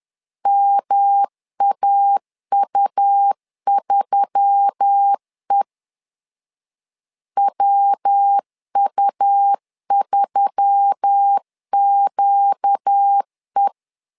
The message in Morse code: MAU3E WU3QE

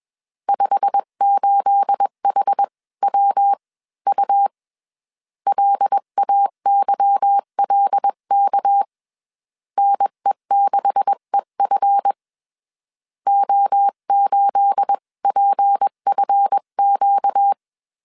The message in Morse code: 58HWU LAYLK NE6EF O8PFQ

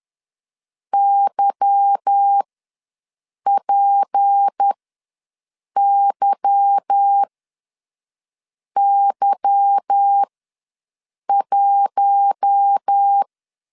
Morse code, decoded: YPYY1

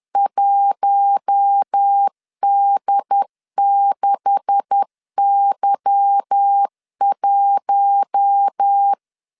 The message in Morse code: 1D6Y1

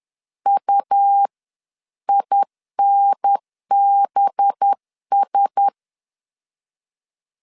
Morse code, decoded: UINBS

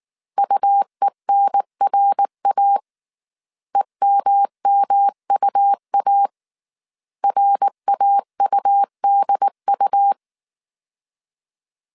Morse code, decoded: UENRA EMMUA RAUDU

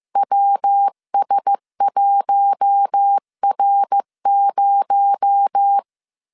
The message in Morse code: WS1R0

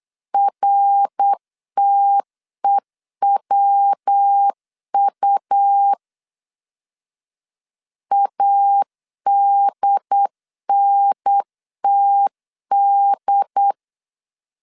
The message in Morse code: RTEWU ADNTD

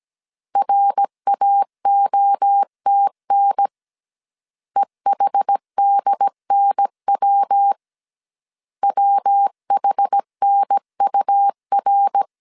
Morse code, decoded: RAOTN EHDNW WHNUR

